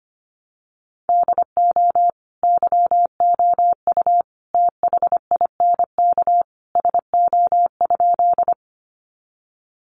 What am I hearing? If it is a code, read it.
Morse code, heard as a keyed tone, about 25 wpm, 700 Hz. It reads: DO YOU THINK SO?
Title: Sentences from Top 200 Words 25wpm